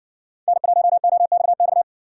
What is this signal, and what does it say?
Morse code, heard as a keyed tone, about 60 wpm, 700 Hz. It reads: N1OXX